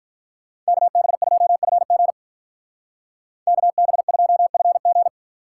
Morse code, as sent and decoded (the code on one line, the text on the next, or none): -.- -... ..--- ..-. --. / -.- -... ..--- ..-. --.
KB2FG KB2FG